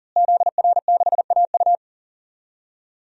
Morse code, heard as a keyed tone, about 40 wpm, 700 Hz. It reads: ZR6AU
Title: DXCC-100 Call Signs - 5 Characters - Repeated 3x - 40wpm